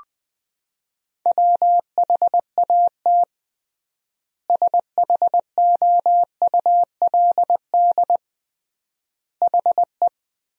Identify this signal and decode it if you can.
Morse code, heard as a keyed tone, about 21 wpm, 700 Hz. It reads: WHAT SHOULD HE
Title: 3x Word-Spacing - Sentences from Top 300 Words - 20wpm